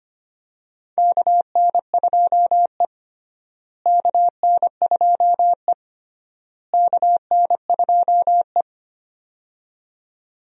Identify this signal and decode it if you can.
Morse code, heard as a keyed tone, about 25 wpm, 700 Hz. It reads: KN2E KN2E KN2E
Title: USA Call Signs - 4 Characters - Repeated 3x - 25wpm